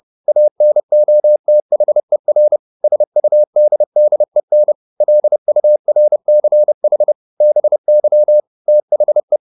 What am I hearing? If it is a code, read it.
Morse code, heard as a keyed tone, about 30 wpm, 600 Hz. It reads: ANOTHER SUDDEN LURCH BY THE
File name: the old man and the sea / the old man and the sea 1 30wpm